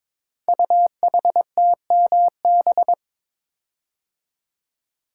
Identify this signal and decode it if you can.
Morse code, heard as a keyed tone, about 22 wpm, 700 Hz. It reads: UHTMB
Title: Code Groups - 5 Letter:Numbers - 22wpm